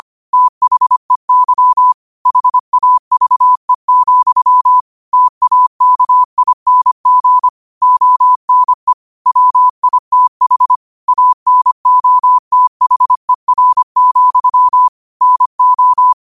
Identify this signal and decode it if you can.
Morse code, heard as a keyed tone, about 25 wpm, 1000 Hz. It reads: THEY HAVE, TAKING ONE WITH ANOTHER, NO
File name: qotdfiles / 25wpm